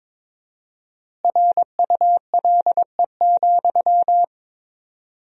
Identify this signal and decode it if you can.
Morse code, heard as a keyed tone, about 22 wpm, 700 Hz. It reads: RULE,
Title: 4x Word-Spacing - Sentences from Top 400 Words - 22wpm